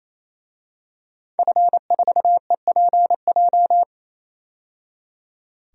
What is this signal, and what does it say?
Morse code, heard as a keyed tone, about 28 wpm, 700 Hz. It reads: F4EPJ
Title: DXCC-100 Call Signs - 5 Characters - 28wpm